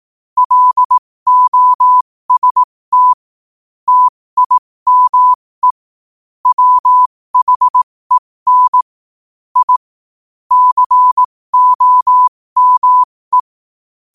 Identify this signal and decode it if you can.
Morse code, heard as a keyed tone, about 18 wpm, spaced as for 15 wpm, 1000 Hz. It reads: LOST TIME WHEN I COME
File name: qotdfiles / 15wpm